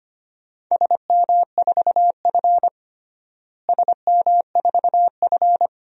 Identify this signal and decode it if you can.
Morse code, heard as a keyed tone, about 25 wpm, 700 Hz. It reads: SM4F SM4F
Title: DXCC-100 Call Signs - 4 Characters - Repeated 2x - 25wpm